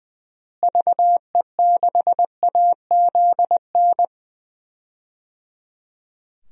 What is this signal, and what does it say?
Morse code, heard as a keyed tone, about 20 wpm, 700 Hz. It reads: VE6AZN
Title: DXCC-100 Call Signs - 6 Characters - Repeated 3x - 20wpm